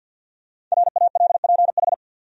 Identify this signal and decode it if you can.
Morse code, heard as a keyed tone, about 48 wpm, 700 Hz. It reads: RALPH
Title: USA Common Names - 50wpm